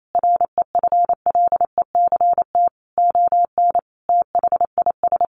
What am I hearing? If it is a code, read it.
Morse code, heard as a keyed tone, about 28 wpm, 700 Hz. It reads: REFLECT ON THIS
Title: Sentences from Top 1500 Words 28wpm